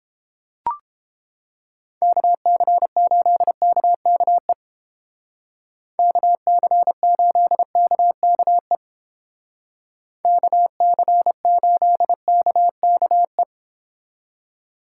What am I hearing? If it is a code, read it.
Morse code, heard as a keyed tone, about 30 wpm, 700 Hz. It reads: KC8KKE KC8KKE KC8KKE